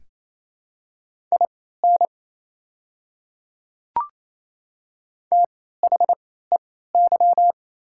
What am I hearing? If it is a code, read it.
Morse code, heard as a keyed tone, about 28 wpm, spaced as for 16 wpm, 690 Hz. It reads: IN THEY
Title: 3x Character Spacing - Top 100 Words - 28wpm